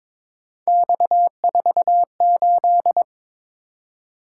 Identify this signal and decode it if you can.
Morse code, heard as a keyed tone, about 22 wpm, 700 Hz. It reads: X48
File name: practice / 22wpm / Three Letter-Number Combinations - 22wpm